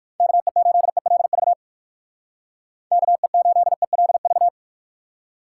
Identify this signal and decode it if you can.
Morse code, heard as a keyed tone, about 59 wpm, 700 Hz. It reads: KE8ELV KE8ELV